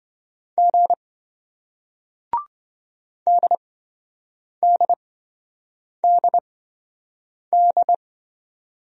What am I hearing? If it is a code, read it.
Morse code, heard as a keyed tone, about 28 wpm, spaced as for 7 wpm, 700 Hz. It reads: G DDDD